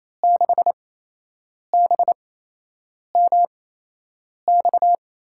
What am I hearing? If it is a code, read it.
Morse code, heard as a keyed tone, about 28 wpm, spaced as for 8 wpm, 700 Hz. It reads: 6BMX
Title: USA Call Signs - 5 Characters - 8x Character Spacing - 28wpm